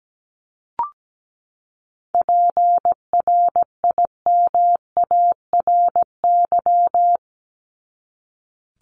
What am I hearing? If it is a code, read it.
Morse code, heard as a keyed tone, about 17 wpm, 700 Hz. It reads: PRIMARY
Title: LY Suffix - Top 100 Words with Base Words 17wpm